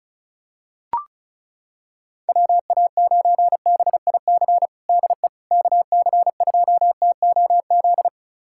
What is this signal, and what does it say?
Morse code, heard as a keyed tone, about 34 wpm, 700 Hz. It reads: WA9BIC DE KC2TOZ
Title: USA Callsign DE Callsign 35wpm